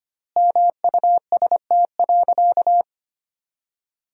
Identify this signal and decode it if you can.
Morse code, heard as a keyed tone, about 25 wpm, 700 Hz. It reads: MUST.
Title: 3x Word-Spacing - Sentences from Top 200 Words - 25wpm